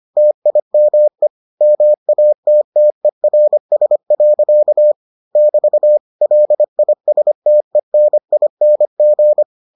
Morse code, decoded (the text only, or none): TIME MATTERS. <BT> LISTENING